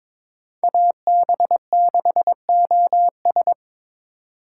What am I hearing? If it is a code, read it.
Morse code, heard as a keyed tone, about 26 wpm, 700 Hz. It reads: AB6OS
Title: Speed Racing - USA Call Signs 22wpm